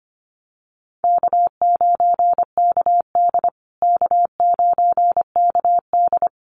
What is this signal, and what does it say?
Morse code, heard as a keyed tone, about 25 wpm, 700 Hz. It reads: K9KD K9KD
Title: USA Call Signs Repeated 3x 25wpm